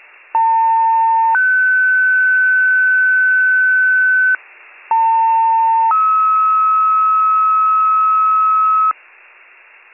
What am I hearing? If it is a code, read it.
Two-tone page, heard as a one-second tone, then a three-second tone, about 900 then 1500 Hz, about 900 then 1290 Hz, with noise. It is heard from 0.3 to 8.9 s.